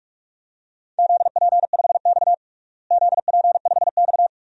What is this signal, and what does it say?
Morse code, heard as a keyed tone, about 46 wpm, 700 Hz. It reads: ZP5X ZP5X
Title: DXCC-100 Call Signs - 4 Characters - Repeated 2x - 45wpm